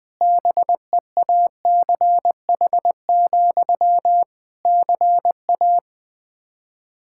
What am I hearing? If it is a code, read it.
Morse code, heard as a keyed tone, about 20 wpm, 700 Hz. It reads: BEACH, CA
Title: Practice QSO - Exchange 1 - Beginning - 20wpm